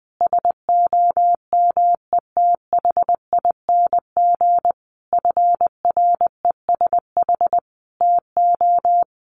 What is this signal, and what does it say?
Morse code, heard as a keyed tone, about 19 wpm, 700 Hz. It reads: SOMETHING FRESH TO
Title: Sentences from Top 1200 Words 20wpm